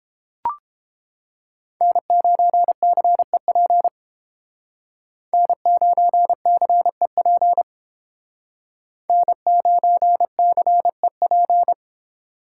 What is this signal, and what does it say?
Morse code, heard as a keyed tone, about 30 wpm, 700 Hz. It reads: N9CEP N9CEP N9CEP